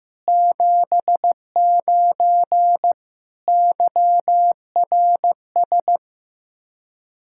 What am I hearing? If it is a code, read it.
Morse code, heard as a keyed tone, about 15 wpm, 700 Hz. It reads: 79 YRS
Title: K1USN SST QSO Practice 15wpm